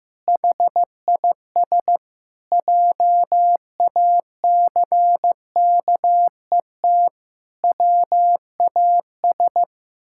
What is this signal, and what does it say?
Morse code, heard as a keyed tone, about 15 wpm, 700 Hz. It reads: HIS JACKET WAS
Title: Pangram Challenge v2 - 15wpm